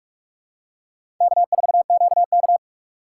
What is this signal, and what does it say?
Morse code, heard as a keyed tone, about 42 wpm, 700 Hz. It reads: K4QK
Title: USA Call Signs - 4 Characters - 3x Character Spacing - 45wpm